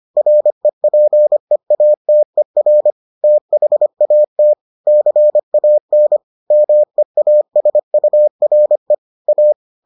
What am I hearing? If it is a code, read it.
Morse code, heard as a keyed tone, about 25 wpm, 600 Hz. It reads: REPEATER THAT CAN MEASURE A